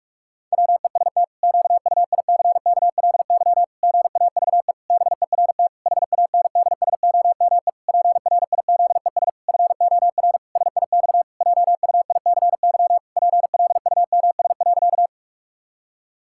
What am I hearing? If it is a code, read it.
Morse code, heard as a keyed tone, about 41 wpm, 700 Hz. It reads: WEST QUICKLY GAVE BERT HANDSOME PRIZES FOR SIX JUICY PLUMS.
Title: Pangram Challenge v2 - 45wpm